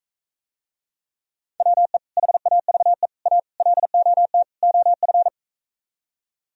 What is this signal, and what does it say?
Morse code, heard as a keyed tone, about 45 wpm, 700 Hz. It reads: WE HAVE A LOT OF